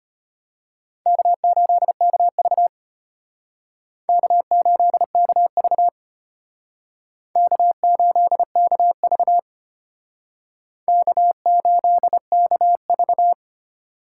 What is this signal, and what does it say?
Morse code, heard as a keyed tone, about 32 wpm, 700 Hz. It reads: K8KV K8KV K8KV K8KV